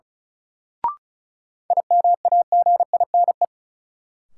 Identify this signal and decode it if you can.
Morse code, heard as a keyed tone, about 34 wpm, 700 Hz. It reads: IMAGINE
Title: Sentences from Top 1400 Words - New Word Drill - 35wpm